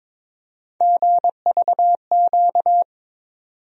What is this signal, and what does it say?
Morse code, heard as a keyed tone, about 22 wpm, 700 Hz. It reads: GVQ
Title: USA Call Sign Postfixes - 3 Letters 22wpm